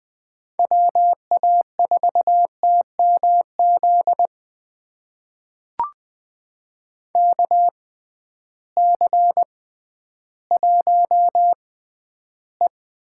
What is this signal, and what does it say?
Morse code, heard as a keyed tone, about 20 wpm, spaced as for 12 wpm, 700 Hz. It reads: WA4TMZ KC1E
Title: USA Call Signs - 6 Characters - 6x Character Spacing - 20wpm